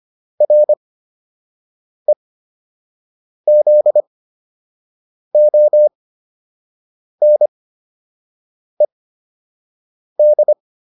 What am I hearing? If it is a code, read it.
Morse code, heard as a keyed tone, about 25 wpm, spaced as for 6 wpm, 600 Hz. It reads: REZONED